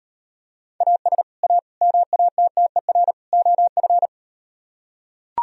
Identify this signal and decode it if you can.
Morse code, heard as a keyed tone, about 38 wpm, 690 Hz. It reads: AS A MATTER OF